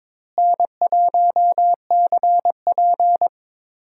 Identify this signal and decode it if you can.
Morse code, heard as a keyed tone, about 22 wpm, 700 Hz. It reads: N1CP
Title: USA Call Signs - 4 Characters - Repeated 2x - 22wpm